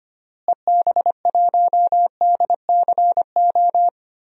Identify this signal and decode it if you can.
Morse code, heard as a keyed tone, about 25 wpm, 700 Hz. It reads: EB1DCO
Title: DXCC-100 Call Signs - 6 Characters - Repeated 3x - 25wpm